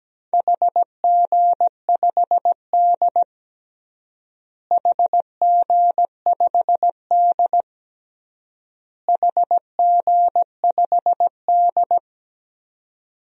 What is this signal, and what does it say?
Morse code, heard as a keyed tone, about 17 wpm, 700 Hz. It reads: HG5D HG5D HG5D
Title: DXCC-100 Call Signs - 4 Characters - Repeated 3x - 17wpm